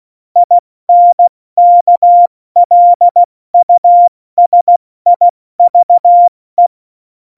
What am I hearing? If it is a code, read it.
Morse code, heard as a keyed tone, about 16 wpm, spaced as for 14 wpm, 700 Hz. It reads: INKLUSIVE
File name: Morsekurs / Witze / Witz 17. November 2020, 70 F80